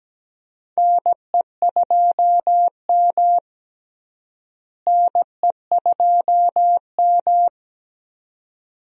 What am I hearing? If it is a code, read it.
Morse code, heard as a keyed tone, about 17 wpm, 700 Hz. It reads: NE2M NE2M
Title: USA Call Signs - 4 Characters - Repeated 2x - 17wpm